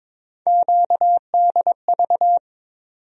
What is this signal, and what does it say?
Morse code, heard as a keyed tone, about 21 wpm, 700 Hz. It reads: QDV
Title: USA Call Sign Postfixes - 3 Letters 22wpm